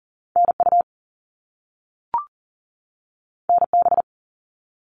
Morse code, heard as a keyed tone, about 40 wpm, 700 Hz. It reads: NU NB
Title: Canadian Provinces and Territories 40wpm